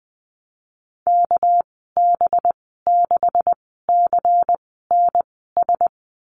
Morse code, heard as a keyed tone, about 19 wpm, spaced as for 14 wpm, 700 Hz. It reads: KB6CNS
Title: 200 Lesson 32g - Letter k - Call Signs 20wpm